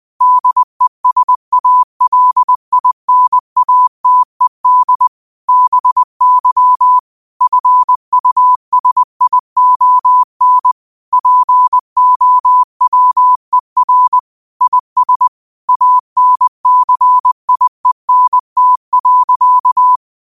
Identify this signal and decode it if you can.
Morse code, heard as a keyed tone, about 20 wpm, 1000 Hz. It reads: DESALINATED BY FUSION POWER IS ANCIENT.